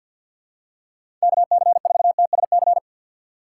Morse code, heard as a keyed tone, about 48 wpm, 700 Hz. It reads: KC4TSC
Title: USA Call Signs - 6 Characters - 3x Character Spacing - 50wpm